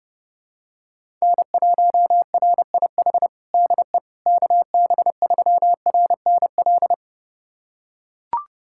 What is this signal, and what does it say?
Morse code, heard as a keyed tone, about 30 wpm, 700 Hz. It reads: N1RIH DE KB3RNL
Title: USA Callsign DE Callsign 30wpm